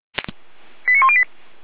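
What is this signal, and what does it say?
Five-tone selective call (ZVEI), heard as quick runs of five tones, tones of about 70 ms. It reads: 89108
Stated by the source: ZVEI five-tone selective call, address 89168, received on an SDR.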